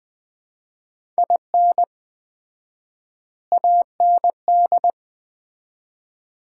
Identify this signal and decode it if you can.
Morse code, heard as a keyed tone, about 20 wpm, 700 Hz. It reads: IN AND
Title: 4x Word-Spacing - Sets of 5 Words - 20wpm